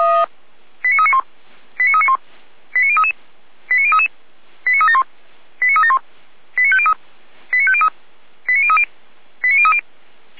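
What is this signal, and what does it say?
Five-tone selective call (ZVEI), heard as quick runs of five tones, tones of about 80 ms. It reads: 89391 89391 89033 89033 89371 89371 89503 89503 89030 89030
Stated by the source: Received on SDR, ZVEI five-tone call 89042.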